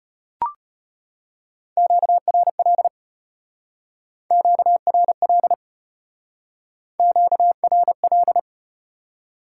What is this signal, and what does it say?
Morse code, heard as a keyed tone, about 32 wpm, 700 Hz. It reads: QRL QRL QRL